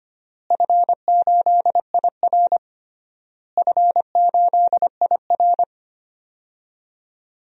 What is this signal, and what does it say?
Morse code, heard as a keyed tone, about 25 wpm, 700 Hz. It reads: F8IR F8IR
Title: DXCC-100 Call Signs - 4 Characters - Repeated 3x - 25wpm